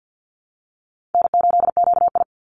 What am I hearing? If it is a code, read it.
Morse code, heard as a keyed tone, about 50 wpm, 700 Hz. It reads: N8QI
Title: USA Call Signs 50wpm